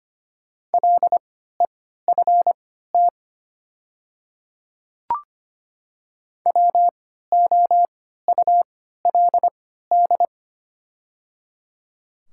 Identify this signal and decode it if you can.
Morse code, heard as a keyed tone, about 25 wpm, spaced as for 14 wpm, 700 Hz. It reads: LEFT WOULD
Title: 3x Character Spacing - Top 300 Words - 25wpm